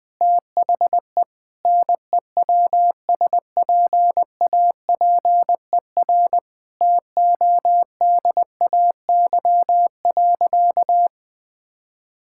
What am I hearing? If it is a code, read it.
Morse code, heard as a keyed tone, about 20 wpm, 700 Hz. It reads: THE NEWSPAPER TODAY.
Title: Sentences from Top 1000 Words - Review Part 6 - 20wpm